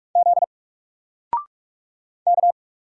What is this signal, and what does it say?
Morse code, heard as a keyed tone, about 44 wpm, 700 Hz. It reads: Z K
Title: Single Character (Letter:Number:Punctuation:Prosign) Alt 45wpm